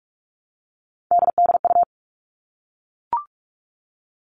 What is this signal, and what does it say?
Morse code, heard as a keyed tone, about 44 wpm, 700 Hz. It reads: DDU
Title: All Cut Numbers - 3-Digit 45wpm